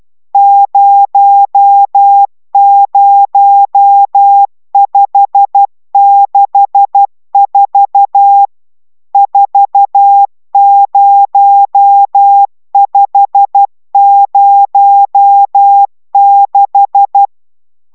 Morse code, heard as a keyed tone, about 12 wpm, 800 Hz. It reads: 00564 40506